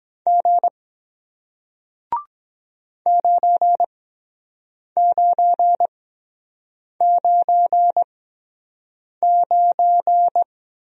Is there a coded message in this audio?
Yes: Morse code from 0.3 to 10.4 s, heard as a keyed tone, about 23 wpm, spaced as for 7 wpm, 700 Hz. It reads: G 9999